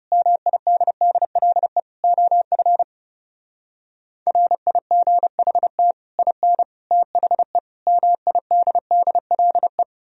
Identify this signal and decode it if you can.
Morse code, heard as a keyed tone, about 37 wpm, 700 Hz. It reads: MIDDLE OF RIGHT IN THE MIDDLE
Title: Speed-Racing - Sets of 5 Words 30wpm